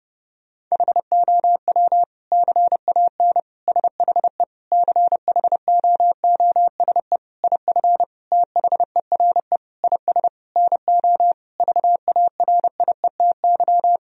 Morse code, heard as a keyed tone, about 30 wpm, 700 Hz. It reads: HOW CAN SHE CHOOSE IF THERE IS NO VARIETY
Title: Sentences from Top 1000 Words - Review Part 4 - 30wpm